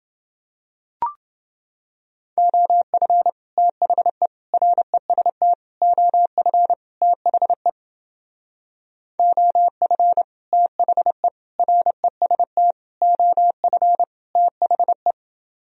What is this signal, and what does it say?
Morse code, heard as a keyed tone, about 29 wpm, 700 Hz. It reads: OF THE REST OF THE OF THE REST OF THE